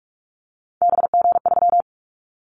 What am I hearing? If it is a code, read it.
Morse code, heard as a keyed tone, about 44 wpm, 700 Hz. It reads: BG3
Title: Callsign Prefixes for Top 100 DXCC - 45wpm